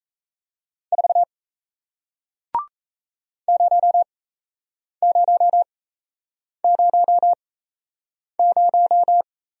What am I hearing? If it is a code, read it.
Morse code, heard as a keyed tone, about 37 wpm, spaced as for 8 wpm, 700 Hz. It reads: 4 0000